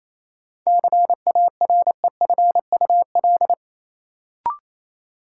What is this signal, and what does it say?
Morse code, heard as a keyed tone, about 28 wpm, 700 Hz. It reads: CAREFUL
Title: Sentences from Top 1600 Words - New Word Drill Alt - 28wpm